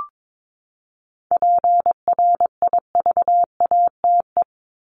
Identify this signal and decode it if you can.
Morse code, heard as a keyed tone, about 22 wpm, 700 Hz. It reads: PRIVATE
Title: Top 751-1000 Words 22wpm